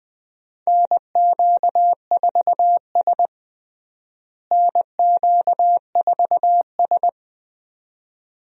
Morse code, heard as a keyed tone, about 20 wpm, 700 Hz. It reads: NQ4S NQ4S
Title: USA Call Signs - 4 Characters - Repeated 2x - 20wpm